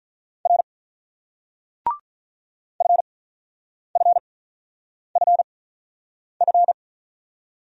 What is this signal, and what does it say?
Morse code, heard as a keyed tone, about 45 wpm, spaced as for 9 wpm, 700 Hz. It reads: R FFFF